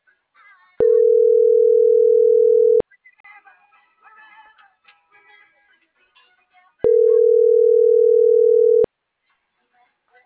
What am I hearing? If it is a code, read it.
Ringback tone (North America), 440 + 480 Hz.
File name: CENTRAL NMP / out itzel.arias